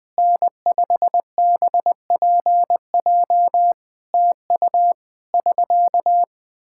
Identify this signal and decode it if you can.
Morse code, heard as a keyed tone, about 20 wpm, 700 Hz. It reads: N5BPJ TU <SK>
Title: Practice QSO - Exchange 4 - Ending - 20wpm